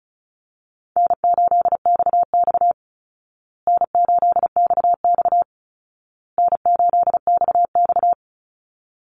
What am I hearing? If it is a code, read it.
Morse code, heard as a keyed tone, about 34 wpm, 700 Hz. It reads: N8XX N8XX N8XX
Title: Call Signs Repeated 2x - 35wpm